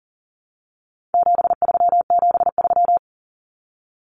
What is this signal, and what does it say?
Morse code, heard as a keyed tone, about 40 wpm, 700 Hz. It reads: 7373
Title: Common Cut Numbers - 4-Digit 40wpm